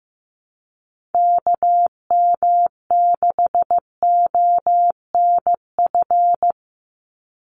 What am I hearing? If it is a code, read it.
Morse code, heard as a keyed tone, about 15 wpm, 700 Hz. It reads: KM6ONF
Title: USA Call Signs 15wpm